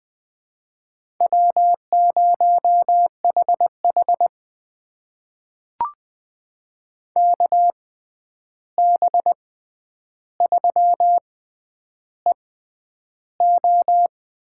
Morse code, heard as a keyed tone, about 20 wpm, spaced as for 10 wpm, 700 Hz. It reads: W0HH KB3EO
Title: USA Call Signs - 6x Character Spacing - 20wpm